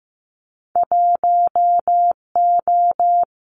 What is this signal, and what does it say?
Morse code, heard as a keyed tone, about 15 wpm, 700 Hz. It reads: 1O
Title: Winter Field Day Exchanges 15wpm